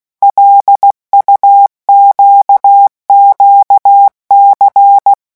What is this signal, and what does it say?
Morse code, heard as a keyed tone, about 16 wpm, 800 Hz. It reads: LUQQC
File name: morse mp3 / revision5-vitesse 15 mots